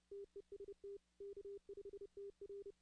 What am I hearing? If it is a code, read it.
Morse code, heard as a keyed tone, about 30 wpm, 400 Hz. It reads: TEST K5TR
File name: 2016-ARRL-DX-CW